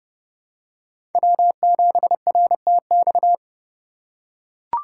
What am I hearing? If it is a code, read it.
Morse code, heard as a keyed tone, about 30 wpm, 700 Hz. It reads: W7RTX
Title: USA Call Signs - 5 Characters - 6x Character Spacing - 30wpm